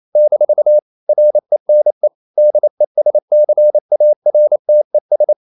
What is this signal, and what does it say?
Morse code, heard as a keyed tone, about 28 wpm, 600 Hz. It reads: - RENE DESCARTES
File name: CW / Practice / EnglishQuotations0128